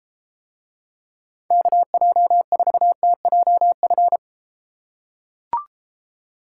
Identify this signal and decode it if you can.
Morse code, heard as a keyed tone, about 33 wpm, 700 Hz. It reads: KJ4TJF